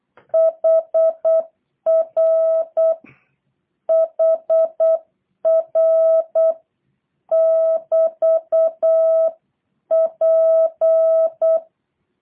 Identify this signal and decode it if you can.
Morse code, heard as a keyed tone, about 8 wpm, 660 Hz. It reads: HR HR <BT> P
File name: jh8cbh / 2023cw / review1-5